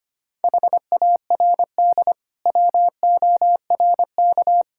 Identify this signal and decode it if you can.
Morse code, heard as a keyed tone, about 25 wpm, 700 Hz. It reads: HARD WORK
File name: practice / 25wpm / Difficult Sentences with Hints - 25wpm